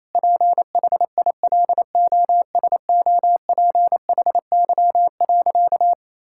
Morse code, heard as a keyed tone, about 27 wpm, 700 Hz. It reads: PHILOSOPHY.
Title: Sentences from Top 1000 Words - Review Part 6 - 28wpm